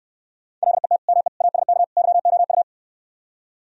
Morse code, heard as a keyed tone, about 69 wpm, 700 Hz. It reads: LET ME ASK YOU